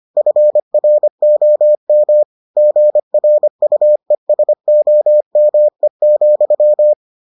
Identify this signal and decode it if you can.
Morse code, heard as a keyed tone, about 25 wpm, 600 Hz. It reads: FROM GRUESOME,